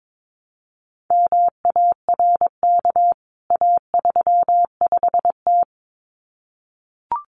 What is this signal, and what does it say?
Morse code, heard as a keyed tone, about 22 wpm, 700 Hz. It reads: MARK A35T
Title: CWT Exchange 22wpm